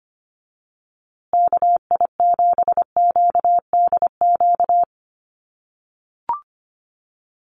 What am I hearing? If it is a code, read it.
Morse code, heard as a keyed tone, about 25 wpm, 700 Hz. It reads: KI7QDQ
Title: USA Call Signs Repeated 2x 25wpm